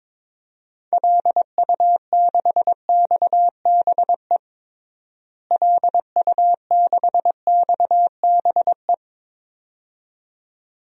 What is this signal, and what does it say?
Morse code, heard as a keyed tone, about 22 wpm, 700 Hz. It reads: LU6XBE LU6XBE
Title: DXCC-100 Call Signs - 6 Characters - Repeated 2x - 22wpm